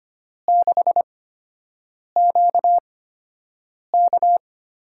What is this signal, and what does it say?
Morse code, heard as a keyed tone, about 25 wpm, spaced as for 7 wpm, 700 Hz. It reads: 6QK
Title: USA Call Signs - 8x Character Spacing - 25wpm